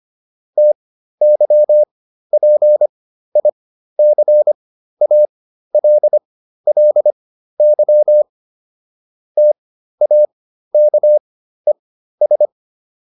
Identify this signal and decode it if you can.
Morse code, heard as a keyed tone, about 25 wpm, spaced as for 13 wpm, 600 Hz. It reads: TYPICALLY TAKES